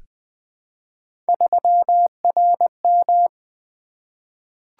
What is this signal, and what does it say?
Morse code, heard as a keyed tone, about 20 wpm, 700 Hz. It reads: 3RM
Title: Three Letter-Number Combinations - 20wpm